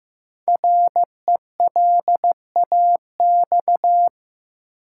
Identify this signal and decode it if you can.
Morse code, heard as a keyed tone, about 15 wpm, 700 Hz. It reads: RELAX
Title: Sentences from Top 1400 Words - New Word Drill Alt - 15wpm